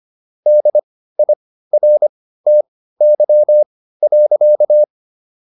Morse code, heard as a keyed tone, about 25 wpm, spaced as for 15 wpm, 600 Hz. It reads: DIRTY.